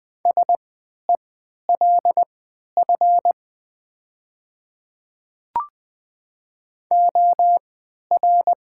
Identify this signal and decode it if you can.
Morse code, heard as a keyed tone, about 20 wpm, spaced as for 11 wpm, 700 Hz. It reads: SELF OR
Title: 3x Character Spacing - Top 300 Words - 20wpm